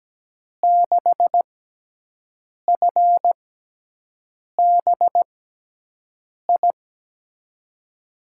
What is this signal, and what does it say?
Morse code, heard as a keyed tone, about 17 wpm, spaced as for 6 wpm, 700 Hz. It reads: 6FBI
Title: USA Call Signs - 6 Characters - 6x Character Spacing - 17wpm